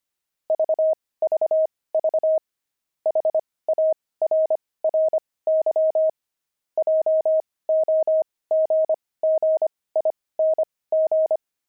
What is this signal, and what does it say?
Morse code, heard as a keyed tone, about 25 wpm, spaced as for 18 wpm, 630 Hz. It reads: VVV HARRY JOGGING
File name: pangram-fs-18-cs-25-monks